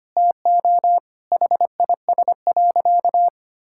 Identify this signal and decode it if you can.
Morse code, heard as a keyed tone, about 25 wpm, 700 Hz. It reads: TO HIS.
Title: Sentences from Top 1000 Words - Review Part 6 - 25wpm